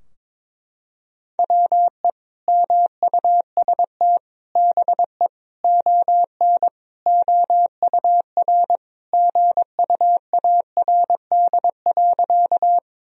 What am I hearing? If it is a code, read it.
Morse code, heard as a keyed tone, about 22 wpm, 700 Hz. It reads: WE MUST BE ON OUR GUARD.
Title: Sentences from Top 1000 Words - Review Part 4 - 22wpm